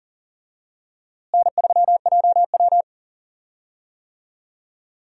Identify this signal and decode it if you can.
Morse code, heard as a keyed tone, about 40 wpm, 700 Hz. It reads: N3JW
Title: USA Call Signs - 4 Characters - Repeated 3x - 40wpm